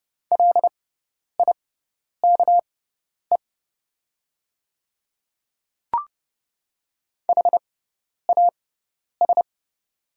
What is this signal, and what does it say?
Morse code, heard as a keyed tone, about 30 wpm, spaced as for 10 wpm, 700 Hz. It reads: LIKE HAS
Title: 6x Character Spacing - Top 100 Words - 30wpm